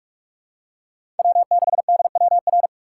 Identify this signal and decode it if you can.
Morse code, heard as a keyed tone, about 43 wpm, 700 Hz. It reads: W6DWR
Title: USA Call Signs - 5 Characters - 6x Character Spacing - 45wpm